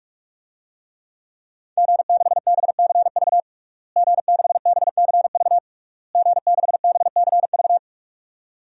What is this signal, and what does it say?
Morse code, heard as a keyed tone, about 41 wpm, 700 Hz. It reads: G6BCV G6BCV G6BCV
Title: DXCC-100 Call Signs - 5 Characters - Repeated 3x - 45wpm